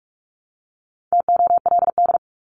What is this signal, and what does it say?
Morse code, heard as a keyed tone, about 44 wpm, 700 Hz. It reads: TOLD
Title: Top 500 Words 45wpm